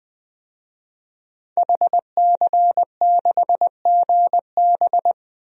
Morse code, heard as a keyed tone, about 21 wpm, 700 Hz. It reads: HC6GB
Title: DXCC-100 Call Signs - 5 Characters - Repeated 2x - 20wpm